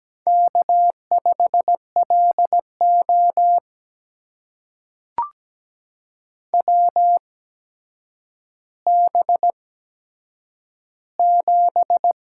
Morse code, heard as a keyed tone, about 17 wpm, spaced as for 8 wpm, 700 Hz. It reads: K5LO WB7
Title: USA Call Signs - 4 Characters - 8x Character Spacing - 17wpm